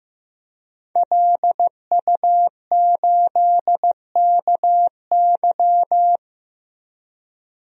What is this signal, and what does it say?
Morse code, heard as a keyed tone, about 15 wpm, 700 Hz. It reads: LU8KY
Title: DXCC-100 Call Signs - 5 Characters - 15wpm